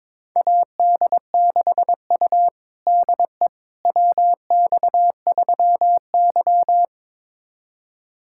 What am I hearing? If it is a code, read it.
Morse code, heard as a keyed tone, about 21 wpm, 700 Hz. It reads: AD6U DE WX3Y
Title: USA Callsign DE Callsign - 4 Characters - 22wpm